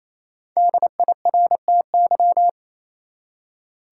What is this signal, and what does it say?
Morse code, heard as a keyed tone, about 28 wpm, 700 Hz. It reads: DIRTY
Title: Five Letter Words - 28wpm